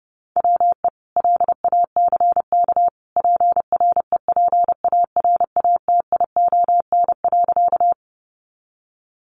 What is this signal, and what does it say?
Morse code, heard as a keyed tone, about 30 wpm, 700 Hz. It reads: WE LACK PREPARATION.
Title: Sentences from Top 1400 Words 30wpm